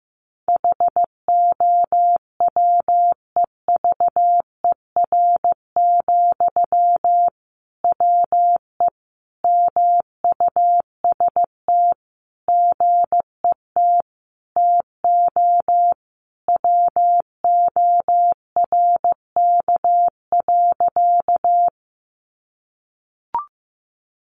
Morse code, heard as a keyed tone, about 15 wpm, 700 Hz. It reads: HOWEVER, WE MUST GET TO WORK.
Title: Sentences from Top 600 Words 15wpm